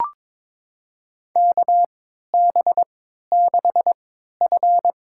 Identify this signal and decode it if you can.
Morse code, heard as a keyed tone, about 22 wpm, spaced as for 13 wpm, 700 Hz. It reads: KB6F
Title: USA Call Signs - 3x Character Spacing - 22wpm